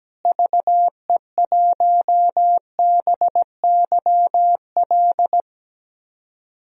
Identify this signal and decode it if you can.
Morse code, heard as a keyed tone, about 17 wpm, 700 Hz. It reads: VE1BYL
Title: DXCC-100 Call Signs - 6 Characters - Repeated 3x - 17wpm